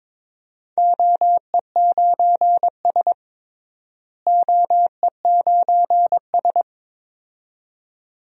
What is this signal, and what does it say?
Morse code, heard as a keyed tone, about 22 wpm, 700 Hz. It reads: OE9S OE9S
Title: DXCC-100 Call Signs - 4 Characters - Repeated 2x - 22wpm